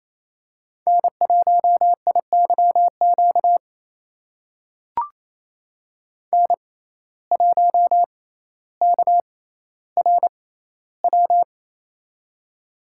Morse code, heard as a keyed tone, about 28 wpm, spaced as for 14 wpm, 700 Hz. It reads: N1IYQ N1KRW
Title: USA Call Signs - 6x Character Spacing - 28wpm